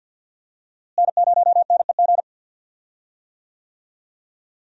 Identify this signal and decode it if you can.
Morse code, heard as a keyed tone, about 52 wpm, 700 Hz. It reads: N0NEG
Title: USA Call Signs - 5 Characters - Repeated 3x - 50wpm